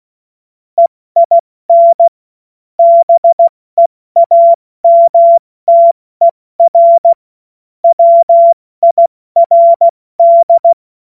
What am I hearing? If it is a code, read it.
Morse code, heard as a keyed tone, about 16 wpm, 680 Hz. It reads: EIN BEAMTER WIRD
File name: Witz-09-September-2025-70-F80